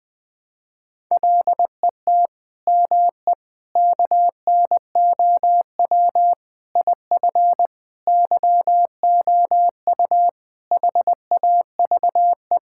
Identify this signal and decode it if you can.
Morse code, heard as a keyed tone, about 20 wpm, 700 Hz. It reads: LET ME KNOW IF YOU HAVE
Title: Sentences with 7-Word Phrases - 20wpm